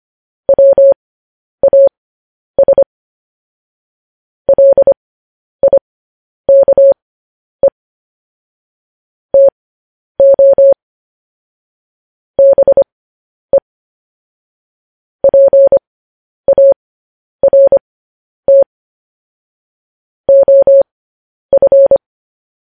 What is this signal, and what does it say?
Morse code, heard as a keyed tone, about 25 wpm, spaced as for 10 wpm, 550 Hz. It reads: WAS LIKE TO BE PART OF
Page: News Headlines in Morse Code at 10 WPM